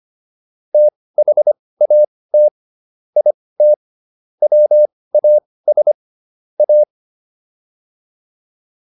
Morse code, heard as a keyed tone, about 25 wpm, spaced as for 18 wpm, 600 Hz. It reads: THAT IT WAS A